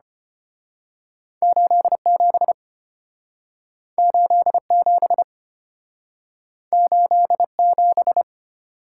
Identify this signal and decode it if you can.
Morse code, heard as a keyed tone, about 34 wpm, 700 Hz. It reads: 87 87 87